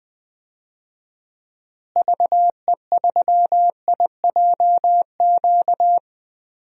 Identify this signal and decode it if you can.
Morse code, heard as a keyed tone, about 20 wpm, 700 Hz. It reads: VE3IJQ
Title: DXCC-100 Call Signs - 6 Characters - Repeated 2x - 20wpm